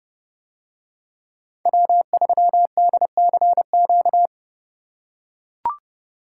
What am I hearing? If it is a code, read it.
Morse code, heard as a keyed tone, about 29 wpm, 700 Hz. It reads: W3DCQ